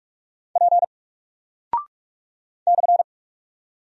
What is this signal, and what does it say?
Morse code, heard as a keyed tone, about 44 wpm, 700 Hz. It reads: P /
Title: Single Character (Letter:Number:Punctuation:Prosign) 45wpm